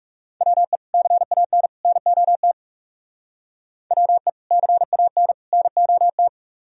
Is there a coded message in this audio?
Yes: Morse code from 0.4 to 6.3 s, heard as a keyed tone, about 39 wpm, 700 Hz. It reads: WE CAN NOT WE CAN NOT